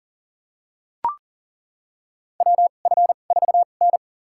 Morse code, heard as a keyed tone, about 40 wpm, spaced as for 29 wpm, 700 Hz. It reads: WF4N
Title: USA Call Signs - 4 Characters - 2x Character Spacing - 40wpm